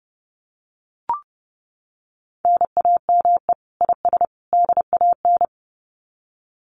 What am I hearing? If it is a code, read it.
Morse code, heard as a keyed tone, about 30 wpm, 700 Hz. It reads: NAME IS DAN
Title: Common QSO Phrases 30wpm